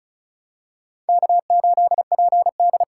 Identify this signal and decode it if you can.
Morse code, heard as a keyed tone, about 34 wpm, 700 Hz. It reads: K8PD
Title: USA Call Signs - 5 Characters - 6x Character Spacing - 35wpm